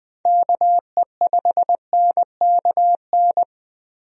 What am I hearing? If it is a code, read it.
Morse code, heard as a keyed tone, about 20 wpm, 700 Hz. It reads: KE5NKN
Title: USA Call Signs - 6 Characters - Repeated 3x - 20wpm